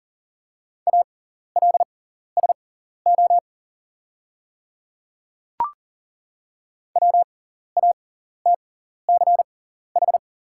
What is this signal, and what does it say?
Morse code, heard as a keyed tone, about 40 wpm, spaced as for 14 wpm, 690 Hz. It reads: ALSO WATCH
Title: 6x Character Spacing - Top 300 Words - 40wpm